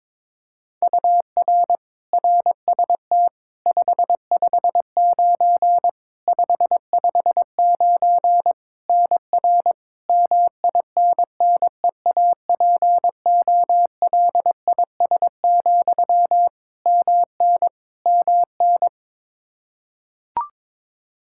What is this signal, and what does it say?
Morse code, heard as a keyed tone, about 22 wpm, 700 Hz. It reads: UR RST 559 559 NR MINNEAPOLIS, MN MN
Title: Practice QSO - Exchange 1 - Beginning - 22wpm